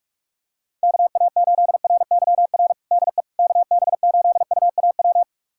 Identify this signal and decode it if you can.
Morse code, heard as a keyed tone, about 46 wpm, 700 Hz. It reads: KA8RYR DE KB8UAW
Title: USA Callsign DE Callsign 45wpm